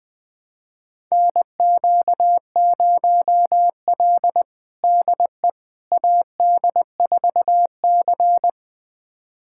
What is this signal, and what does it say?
Morse code, heard as a keyed tone, about 20 wpm, 700 Hz. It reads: NQ0L DE AD4C
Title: USA Callsign DE Callsign - 4 Characters - 20wpm